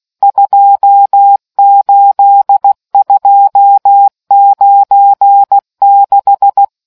Morse code, heard as a keyed tone, about 16 wpm, 800 Hz. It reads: 28296